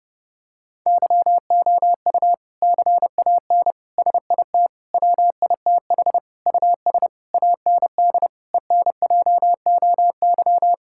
Morse code, heard as a keyed tone, about 30 wpm, 700 Hz. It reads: YOU CAN SIT WITH US AND ENJOY
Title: Sentences from Top 1000 Words - Review Part 5 - 30wpm